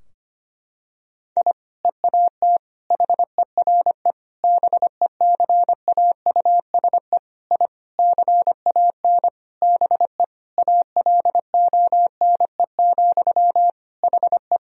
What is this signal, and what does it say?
Morse code, heard as a keyed tone, about 25 wpm, 700 Hz. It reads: I EAT HERE BECAUSE I CAN BE ALONE, HE
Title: Sentences from Top 1000 Words - Review Part 3 - 25wpm